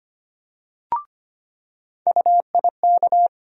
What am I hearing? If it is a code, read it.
Morse code, heard as a keyed tone, about 25 wpm, 700 Hz. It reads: UIK
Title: USA Call Sign Postfixes - 3 Letters 25wpm